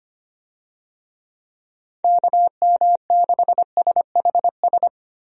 Morse code, heard as a keyed tone, about 25 wpm, 700 Hz. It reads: KM6SHS
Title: USA Call Signs - 6 Characters - Repeated 2x - 25wpm